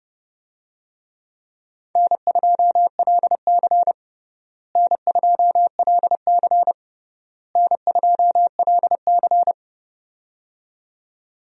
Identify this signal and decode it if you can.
Morse code, heard as a keyed tone, about 30 wpm, 700 Hz. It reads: N2LC N2LC N2LC
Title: USA Call Signs - 4 Characters - Repeated 3x - 30wpm